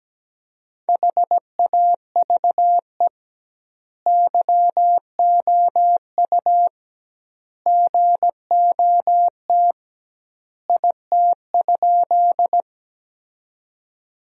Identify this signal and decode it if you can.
Morse code, heard as a keyed tone, about 17 wpm, 700 Hz. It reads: HAVE YOU GOT IT?
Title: 2x Word-Spacing - Sentences from Top 300 Words - 17wpm